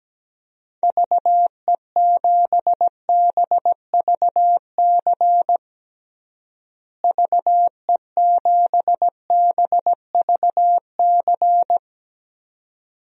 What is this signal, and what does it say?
Morse code, heard as a keyed tone, about 17 wpm, 700 Hz. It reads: VE7BVC VE7BVC
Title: DXCC-100 Call Signs - 6 Characters - Repeated 2x - 17wpm